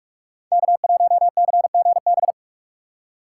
Morse code, heard as a keyed tone, about 41 wpm, 700 Hz. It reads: K1CGB
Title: Code Groups - 5 Letter:Numbers - 45wpm